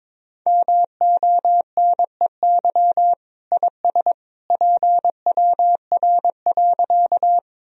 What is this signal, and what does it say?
Morse code, heard as a keyed tone, about 22 wpm, 700 Hz. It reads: MONEY IS PWR.
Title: Sentences from Top 500 Words and Abbreviations 22wpm